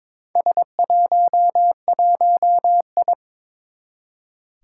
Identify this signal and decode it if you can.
Morse code, heard as a keyed tone, about 22 wpm, 700 Hz. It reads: S11I
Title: Four Challenging Letter-Number Combinations 22wpm